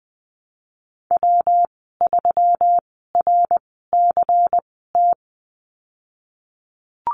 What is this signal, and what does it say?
Morse code, heard as a keyed tone, about 20 wpm, spaced as for 15 wpm, 700 Hz. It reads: W3RCT
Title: 193 Lesson 31g - Letter v - Call Signs 20wpm